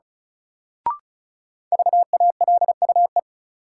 Morse code, heard as a keyed tone, about 36 wpm, 700 Hz. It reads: VALUE
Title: Sentences from Top 600 Words - New Word Drill - 35wpm